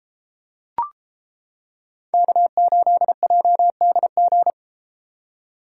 Morse code, heard as a keyed tone, about 31 wpm, 700 Hz. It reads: K8JDG